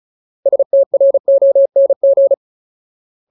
Morse code, heard as a keyed tone, about 34 wpm, 550 Hz. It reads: STRONG
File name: Ultimate Word List / 35WPM Ultimate